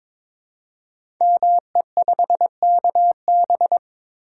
Morse code, heard as a keyed tone, about 22 wpm, 700 Hz. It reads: ME5KB
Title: Code Groups - 5 Letter:Numbers - 22wpm